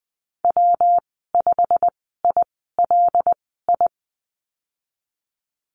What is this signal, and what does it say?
Morse code, heard as a keyed tone, about 20 wpm, spaced as for 15 wpm, 700 Hz. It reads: W5ILI
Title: 154 Lesson 25g - Letter p - Call Signs 20wpm